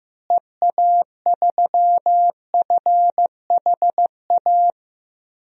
Morse code, heard as a keyed tone, about 15 wpm, 700 Hz. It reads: EA3FHA
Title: DXCC-100 Call Signs - 6 Characters - Repeated 3x - 15wpm